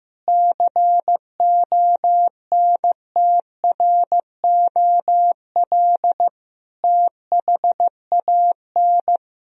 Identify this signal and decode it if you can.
Morse code, heard as a keyed tone, about 15 wpm, 700 Hz. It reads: CONTROL THAN
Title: Sentences from Top 1000 Words - Review Part 3 - 15wpm